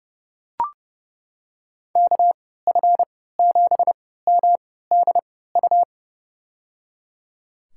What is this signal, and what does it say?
Morse code, heard as a keyed tone, about 30 wpm, spaced as for 17 wpm, 700 Hz. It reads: KF7MDU
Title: USA Call Signs - 6 Characters - 3x Character Spacing - 30wpm